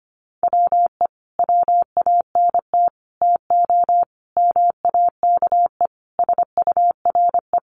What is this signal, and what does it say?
Morse code, heard as a keyed tone, about 25 wpm, 700 Hz. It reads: WE WANT TO MAKE SURE
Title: Sets of 5 Words Encore2 25wpm